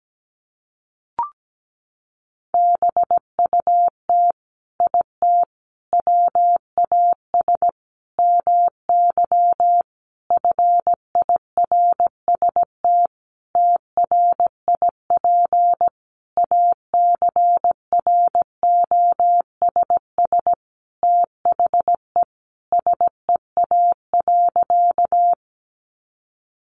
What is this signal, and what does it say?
Morse code, heard as a keyed tone, about 17 wpm, 700 Hz. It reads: BUT IT WAS MY FIRST TRIP ACROSS THE SEA.
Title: Sentences from Top 1300 Words 17wpm